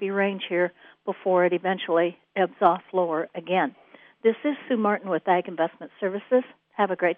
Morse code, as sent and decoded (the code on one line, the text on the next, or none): none